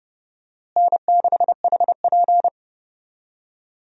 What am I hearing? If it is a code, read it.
Morse code, heard as a keyed tone, about 30 wpm, 700 Hz. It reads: N6HP
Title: USA Call Signs - 4 Characters - Repeated 2x - 30wpm